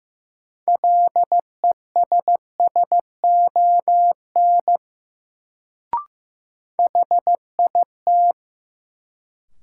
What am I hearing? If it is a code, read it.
Morse code, heard as a keyed tone, about 15 wpm, 700 Hz. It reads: LESSON HIT
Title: Sentences from Top 1200 Words - New Word Drill - 15wpm